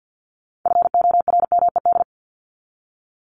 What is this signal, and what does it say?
Morse code, heard as a keyed tone, about 50 wpm, 700 Hz. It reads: FORMED
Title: ED Suffix - Top 100 Words with Base Words 50wpm